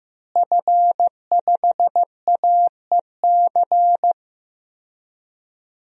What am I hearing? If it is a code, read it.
Morse code, heard as a keyed tone, about 16 wpm, 700 Hz. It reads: F5AEC
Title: DXCC-100 Call Signs - 5 Characters - Repeated 3x - 15wpm